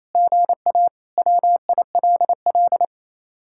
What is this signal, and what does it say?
Morse code, heard as a keyed tone, about 32 wpm, 700 Hz. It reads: GA WILL
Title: Speed-Racing Common QSO Phrases Encore - 28wpm